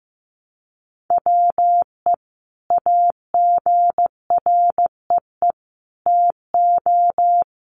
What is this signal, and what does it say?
Morse code, heard as a keyed tone, about 15 wpm, 700 Hz. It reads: WE AGREE TO
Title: Sentences from Top 1100 Words 15wpm